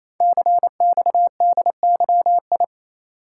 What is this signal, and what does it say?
Morse code, heard as a keyed tone, about 28 wpm, 700 Hz. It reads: CXDYI
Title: Code Groups - 5 Letters - 28wpm